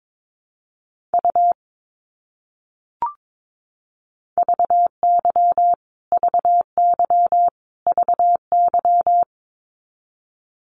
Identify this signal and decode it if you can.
Morse code, heard as a keyed tone, about 22 wpm, 700 Hz. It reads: U VY VY VY
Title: Abbreviations Repeated 3x - 22wpm